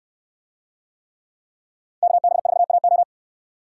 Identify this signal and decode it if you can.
Morse code, heard as a keyed tone, about 64 wpm, 700 Hz. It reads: KB4AO